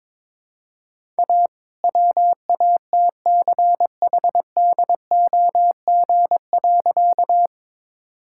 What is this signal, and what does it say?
Morse code, heard as a keyed tone, about 22 wpm, 700 Hz. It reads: A WATCHDOG.
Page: morse-code-ninja-practice-22wpm